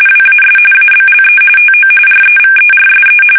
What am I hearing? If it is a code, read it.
RTTY, about 45.45 baud. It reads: WAS ABLE TO REVERSE
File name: fsk850 / usnavy3